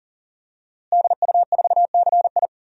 Morse code, heard as a keyed tone, about 40 wpm, 700 Hz. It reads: DU4CI
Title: DXCC-100 Call Signs - 5 Characters - Repeated 2x - 40wpm